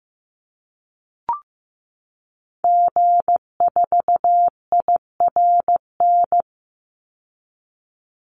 Morse code, heard as a keyed tone, about 15 wpm, 700 Hz. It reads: G4IRN
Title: Call Signs Repeated 3x - 15wpm